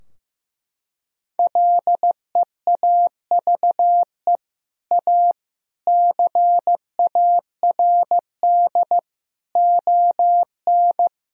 Morse code, heard as a keyed tone, about 15 wpm, 700 Hz. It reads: LEAVE A CARD ON
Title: Sentences from Top 1000 Words - Review Part 4 - 15wpm